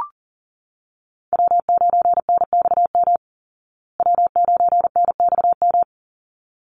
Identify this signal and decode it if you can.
Morse code, heard as a keyed tone, about 40 wpm, 700 Hz. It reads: W9NXM W9NXM
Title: Call Signs Repeated 3x - 40wpm